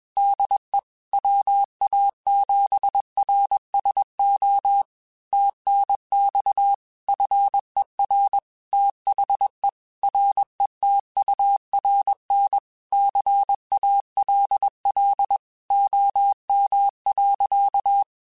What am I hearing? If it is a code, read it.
Morse code, heard as a keyed tone, about 21 wpm, 800 Hz. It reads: DE WA7RSO TNX FER THE RETURN CALL OM.
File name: QSOs / QSOs-mp3s / kc7haa@20